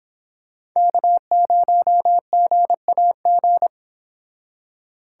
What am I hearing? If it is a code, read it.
Morse code, heard as a keyed tone, about 26 wpm, 700 Hz. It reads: K0GAG